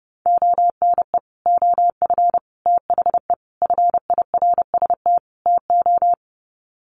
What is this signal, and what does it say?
Morse code, heard as a keyed tone, about 30 wpm, 700 Hz. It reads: ONE OF THE FIRST TO
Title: Sets of 5 Words Encore 30wpm